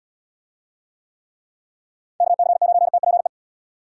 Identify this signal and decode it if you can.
Morse code, heard as a keyed tone, about 75 wpm, 700 Hz. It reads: KC9TPE